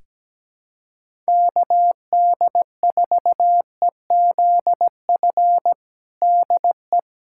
Morse code, heard as a keyed tone, about 17 wpm, 700 Hz. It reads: KD4EZF DE
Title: USA Callsign DE Callsign 17wpm